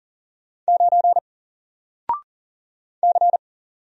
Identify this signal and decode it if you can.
Morse code, heard as a keyed tone, about 39 wpm, 700 Hz. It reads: 9 C
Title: Single Character (Letter:Number:Punctuation:Prosign) Alt 40wpm